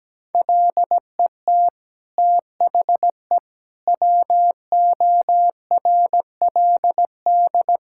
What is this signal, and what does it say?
Morse code, heard as a keyed tone, about 17 wpm, 700 Hz. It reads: LET THE WORLD
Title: Sentences from Top 1600 Words 17wpm